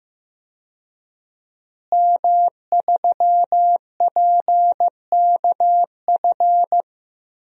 Morse code, heard as a keyed tone, about 15 wpm, 700 Hz. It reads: M3PKF
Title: DXCC-100 Call Signs - 5 Characters - Repeated 3x - 15wpm